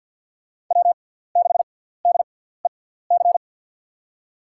Morse code, heard as a keyed tone, about 47 wpm, spaced as for 17 wpm, 700 Hz. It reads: W6DEC
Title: USA Call Signs - 6x Character Spacing - 50wpm